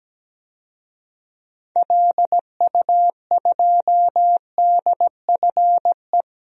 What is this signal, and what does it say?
Morse code, heard as a keyed tone, about 17 wpm, 700 Hz. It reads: LU2DFE
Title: DXCC-100 Call Signs - 6 Characters - 17wpm